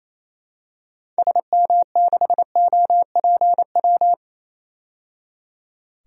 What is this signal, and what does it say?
Morse code, heard as a keyed tone, about 28 wpm, 700 Hz. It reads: SM6OPW
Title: DXCC-100 Call Signs - 6 Characters - 28wpm